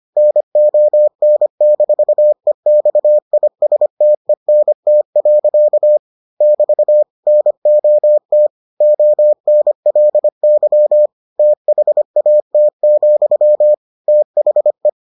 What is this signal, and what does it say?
Morse code, heard as a keyed tone, about 25 wpm, 600 Hz. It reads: NON-EXISTENT. <BT> NOT ONLY THAT, THE